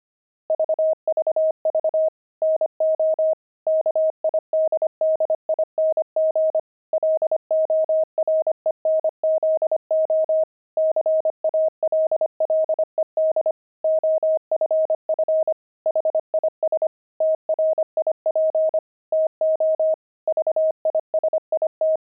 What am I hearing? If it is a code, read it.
Morse code, heard as a keyed tone, about 25 wpm, 630 Hz. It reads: VVV NO KIDDING LORENZO CALLED OFF HIS TRIP TO VISIT
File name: pangram-fs-25-cs-25-kidding